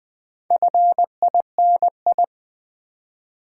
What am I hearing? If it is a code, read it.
Morse code, heard as a keyed tone, about 20 wpm, 690 Hz. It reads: FINI
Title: Build A Word - 2 Letters At A Time - 6-Letter Words - 20wpm